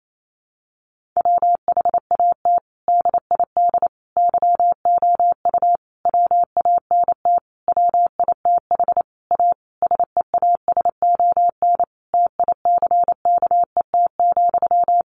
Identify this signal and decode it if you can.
Morse code, heard as a keyed tone, about 28 wpm, 700 Hz. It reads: WHAT DID YOU WANT WITH A SEASON TICKET,
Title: Sentences from Top 1400 Words 28wpm